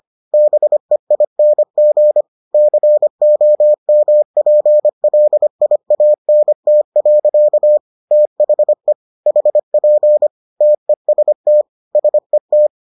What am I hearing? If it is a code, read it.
Morse code, heard as a keyed tone, about 25 wpm, 600 Hz. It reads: BEING COMPLIANT. THE HP TEST SET